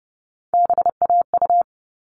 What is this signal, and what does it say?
Morse code, heard as a keyed tone, about 30 wpm, 700 Hz. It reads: BAU
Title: All Cut Numbers - 3-Digit 30wpm